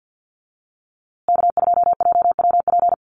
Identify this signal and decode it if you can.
Morse code, heard as a keyed tone, about 49 wpm, 700 Hz. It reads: K2JWP
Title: USA Call Signs Repeated 3x 50wpm